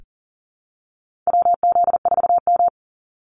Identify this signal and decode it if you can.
Morse code, heard as a keyed tone, about 40 wpm, 700 Hz. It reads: WZ4M
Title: NA SOTA Activator Callsigns 40wpm